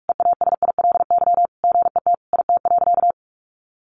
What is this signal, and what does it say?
Morse code, heard as a keyed tone, about 44 wpm, 700 Hz. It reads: EASILY GET IT.
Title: Sentences from Top 1500 Words 45wpm